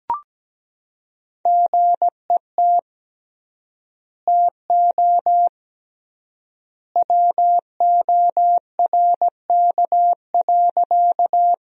Morse code, heard as a keyed tone, about 17 wpm, 700 Hz. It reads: GET TO WORK.
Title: 3x Word-Spacing - Sentences from Top 200 Words - 17wpm